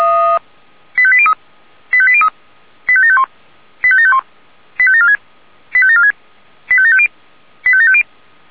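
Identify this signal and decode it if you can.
Five-tone selective call (ZVEI), heard as quick runs of five tones, tones of about 70 ms. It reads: 85803 85803 85751 85751 85746 85746 85750 85750
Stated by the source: ZVEI five-tone selective call 85730, received by SDR.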